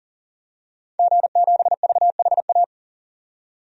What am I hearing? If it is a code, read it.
Morse code, heard as a keyed tone, about 37 wpm, 700 Hz. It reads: G7VHA
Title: DXCC-100 Call Signs - 5 Characters - 40wpm